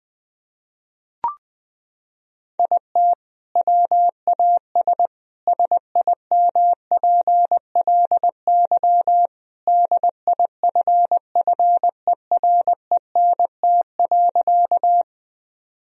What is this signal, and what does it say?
Morse code, heard as a keyed tone, about 20 wpm, 700 Hz. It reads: IT WAS SIMPLY DIFFERENT.
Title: Sentences from Top 1000 Words - Review Part 4 - 20wpm